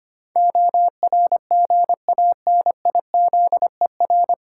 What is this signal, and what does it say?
Morse code, heard as a keyed tone, about 25 wpm, 700 Hz. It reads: ORGANIZER
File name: Nine Letter Words - 25wpm